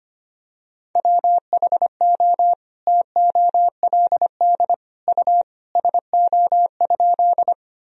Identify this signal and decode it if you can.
Morse code, heard as a keyed tone, about 25 wpm, 700 Hz. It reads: WHO TOLD U SO?
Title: Sentences from Top 400 Words and Abbreviations 25wpm